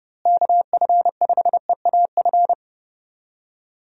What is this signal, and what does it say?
Morse code, heard as a keyed tone, about 30 wpm, 700 Hz. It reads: KF5EAF
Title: USA Call Signs - 6 Characters - Repeated 2x - 30wpm